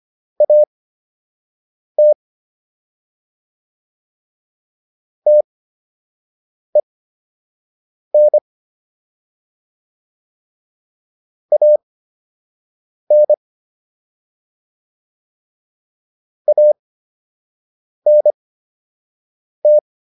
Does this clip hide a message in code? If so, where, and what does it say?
Yes: Morse code from 0.4 to 19.8 s, heard as a keyed tone, about 25 wpm, spaced as for 6 wpm, 600 Hz. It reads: AT TEN AN ANT